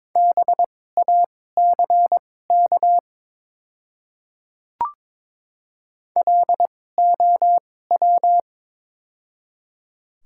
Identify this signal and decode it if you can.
Morse code, heard as a keyed tone, about 22 wpm, spaced as for 16 wpm, 700 Hz. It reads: BACK LOW
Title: 2x Character Spacing - Top 300 Words - 22wpm